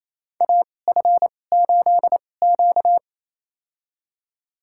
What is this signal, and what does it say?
Morse code, heard as a keyed tone, about 28 wpm, spaced as for 20 wpm, 700 Hz. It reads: AF8Q
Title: USA Call Signs - 4 Characters - 2x Character Spacing - 28wpm